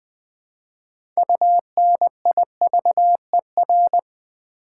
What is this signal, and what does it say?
Morse code, heard as a keyed tone, about 20 wpm, 700 Hz. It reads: UNIVER
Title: Build A Word - 3 Letters At A Time - 8-Letter Words - 20wpm